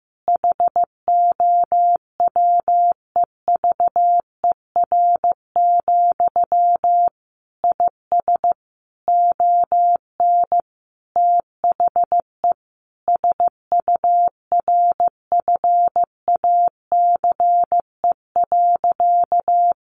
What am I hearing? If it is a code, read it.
Morse code, heard as a keyed tone, about 15 wpm, 700 Hz. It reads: HOWEVER, IS ON THE SURFACE.
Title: Sentences from Top 600 Words 15wpm